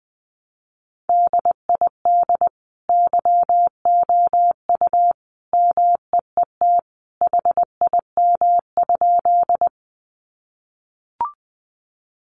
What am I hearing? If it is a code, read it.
Morse code, heard as a keyed tone, about 20 wpm, 700 Hz. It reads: DID YOU MEET HIM?
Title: Sentences from Top 1100 Words 20wpm